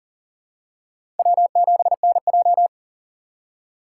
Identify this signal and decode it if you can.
Morse code, heard as a keyed tone, about 40 wpm, 700 Hz. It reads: W7NJ
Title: USA Call Signs - 2x Character Spacing - 40wpm